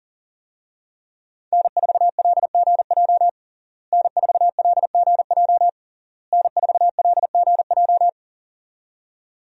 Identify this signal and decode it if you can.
Morse code, heard as a keyed tone, about 40 wpm, 700 Hz. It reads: N4LGJ N4LGJ N4LGJ
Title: USA Call Signs - 5 Characters - Repeated 3x - 40wpm